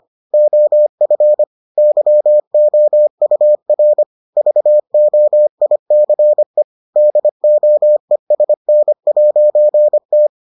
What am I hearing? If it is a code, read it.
Morse code, heard as a keyed tone, about 25 wpm, 600 Hz. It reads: OF YOUR VOICE DOESN'T